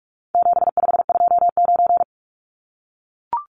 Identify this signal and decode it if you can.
Morse code, heard as a keyed tone, about 44 wpm, 700 Hz. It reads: 7529
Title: Numbers - 4-Digit 45wpm